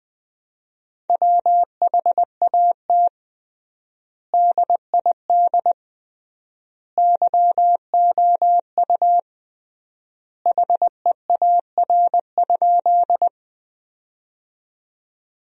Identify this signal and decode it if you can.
Morse code, heard as a keyed tone, about 21 wpm, 700 Hz. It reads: WHAT DID YOU HEAR?
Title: 3x Word-Spacing - Sentences from Top 300 Words - 20wpm